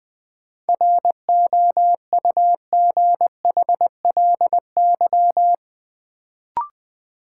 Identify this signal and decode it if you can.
Morse code, heard as a keyed tone, about 21 wpm, 700 Hz. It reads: ROUGHLY
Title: Sentences from Top 1500 Words - New Word Drill Alt - 20wpm